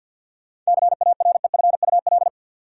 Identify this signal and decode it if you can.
Morse code, heard as a keyed tone, about 51 wpm, 700 Hz. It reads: CAREFUL
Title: Sentences from Top 1600 Words - New Word Drill - 50wpm